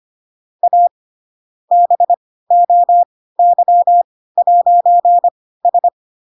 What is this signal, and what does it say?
Morse code, heard as a keyed tone, about 25 wpm, spaced as for 16 wpm, 700 Hz. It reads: A BOY'S
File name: Pinocchio / Pinocchio ch5 10 25 ik1hge